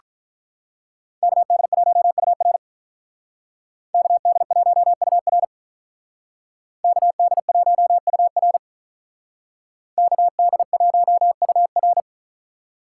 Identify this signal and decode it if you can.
Morse code, heard as a keyed tone, about 45 wpm, 700 Hz. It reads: KD1UR KD1UR KD1UR KD1UR